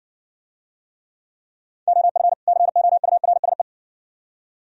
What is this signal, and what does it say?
Morse code, heard as a keyed tone, about 66 wpm, 700 Hz. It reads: OF COURSE